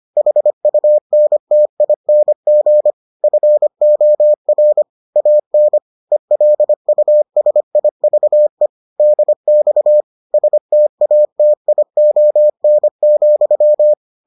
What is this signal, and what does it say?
Morse code, heard as a keyed tone, about 25 wpm, 600 Hz. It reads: HUNTING FOR AN ELUSIVE DX STATION,